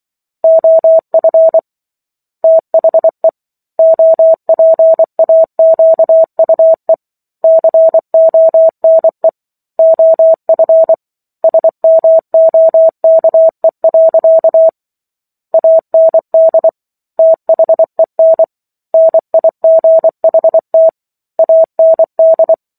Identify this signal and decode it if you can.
Morse code, heard as a keyed tone, about 24 wpm, 650 Hz. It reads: OF THE OPAQUE CONE OF SMOKE. AND THEN NIGHT AND